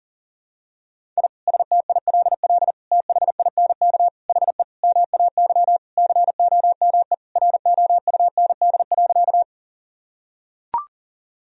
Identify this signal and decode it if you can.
Morse code, heard as a keyed tone, about 40 wpm, 700 Hz. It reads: I STILL THINK HE MAY COME ROUND.
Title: Sentences from Top 1000 Words - Review Part 1 - 40wpm